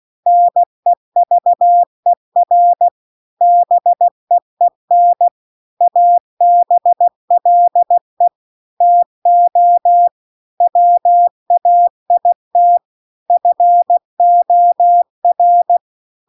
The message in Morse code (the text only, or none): NEVER BEEN ABLE TO WAIT FOR